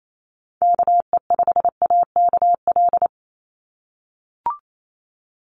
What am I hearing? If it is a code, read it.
Morse code, heard as a keyed tone, about 28 wpm, 700 Hz. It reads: KE5AKL
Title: NA SOTA Chaser Callsigns 28wpm